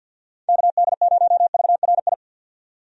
Morse code, heard as a keyed tone, about 51 wpm, 700 Hz. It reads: KD0VRI
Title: USA Call Signs - 6 Characters - 50wpm